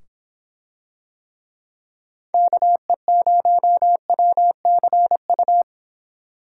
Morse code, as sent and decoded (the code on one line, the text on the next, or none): -.- . ----- .-- -.-. ..-
KE0WCU